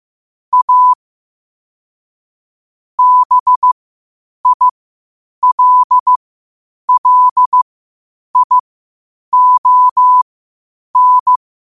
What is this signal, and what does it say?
Morse code, heard as a keyed tone, about 15 wpm, spaced as for 8 wpm, 1000 Hz. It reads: A BILLION
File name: qotdfiles / 8wpm